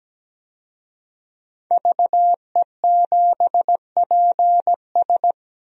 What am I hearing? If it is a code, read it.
Morse code, heard as a keyed tone, about 17 wpm, 700 Hz. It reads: VE7PS
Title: DXCC-100 Call Signs - 5 Characters - 17wpm